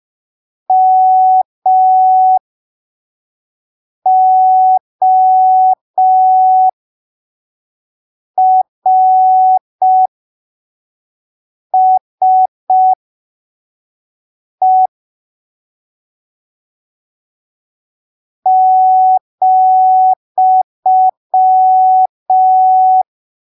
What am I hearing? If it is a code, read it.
Morse code, heard as a keyed tone, about 5 wpm, spaced as for 3 wpm, 750 Hz. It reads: MORSE ,
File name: arquivos / texto / texto 17 05ppm